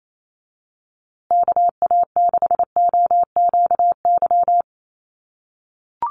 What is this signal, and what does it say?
Morse code, heard as a keyed tone, about 28 wpm, 700 Hz. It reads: KA6OQY
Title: USA Call Signs Repeated 2x 28wpm